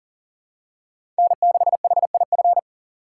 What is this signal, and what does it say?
Morse code, heard as a keyed tone, about 37 wpm, 700 Hz. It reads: N6HIF
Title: USA Call Signs - 5 Characters - 2x Character Spacing - 40wpm